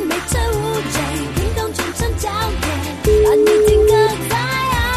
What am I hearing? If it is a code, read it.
Ringback tone (Europe, 425 Hz), with noise.